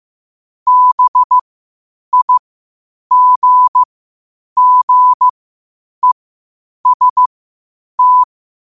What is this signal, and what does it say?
Morse code, heard as a keyed tone, about 15 wpm, spaced as for 8 wpm, 1000 Hz. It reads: BIGGEST
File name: qotdfiles / 8wpm